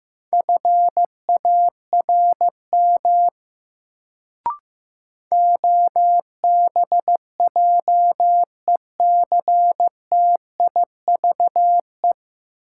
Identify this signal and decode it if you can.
Morse code, heard as a keyed tone, about 15 wpm, 700 Hz. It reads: FARM OBJECTIVE
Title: Sentences from Top 1200 Words - New Word Drill Alt - 15wpm